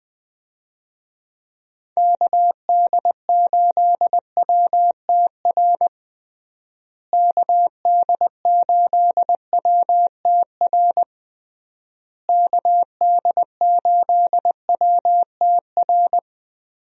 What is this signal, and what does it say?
Morse code, heard as a keyed tone, about 21 wpm, 700 Hz. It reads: KD8WTR KD8WTR KD8WTR
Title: USA Call Signs - 6 Characters - Repeated 3x - 20wpm